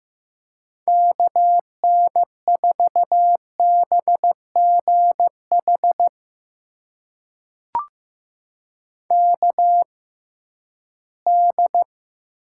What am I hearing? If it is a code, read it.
Morse code, heard as a keyed tone, about 15 wpm, spaced as for 11 wpm, 700 Hz. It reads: KN4BGH KD
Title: USA Call Signs - 6 Characters - 6x Character Spacing - 15wpm